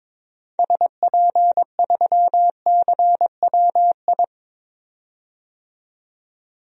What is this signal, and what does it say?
Morse code, heard as a keyed tone, about 22 wpm, 700 Hz. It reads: SP3CWI
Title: DXCC-100 Call Signs - 6 Characters - 22wpm